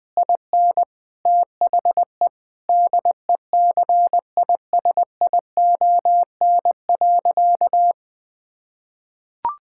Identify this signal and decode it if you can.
Morse code, heard as a keyed tone, about 20 wpm, 700 Hz. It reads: IN THE DECISION.
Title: Sentences with 6-Word Phrases - 20wpm